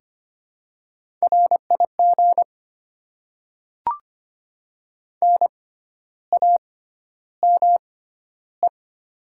Morse code, heard as a keyed tone, about 25 wpm, spaced as for 12 wpm, 700 Hz. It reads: RIG NAME
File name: Top 20 QSO Elements - 6x Character Spacing - 25wpm